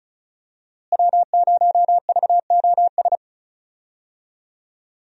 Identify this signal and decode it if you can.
Morse code, heard as a keyed tone, about 36 wpm, 700 Hz. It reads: W0VOS
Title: USA Call Signs - 5 Characters - Repeated 3x - 35wpm